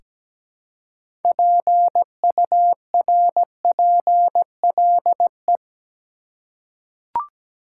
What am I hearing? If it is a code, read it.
Morse code, heard as a keyed tone, about 17 wpm, 700 Hz. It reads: PURPLE
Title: Six Letter Words - 17wpm